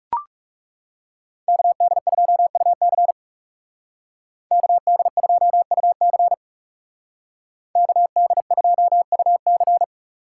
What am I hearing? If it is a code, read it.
Morse code, heard as a keyed tone, about 36 wpm, 700 Hz. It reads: KD2UC KD2UC KD2UC
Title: Speed Racing - USA Call Signs 30wpm